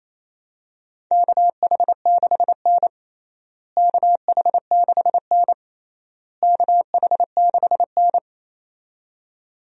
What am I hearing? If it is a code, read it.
Morse code, heard as a keyed tone, about 28 wpm, 700 Hz. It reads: KH6N KH6N KH6N
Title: DXCC-100 Call Signs - 4 Characters - Repeated 3x - 28wpm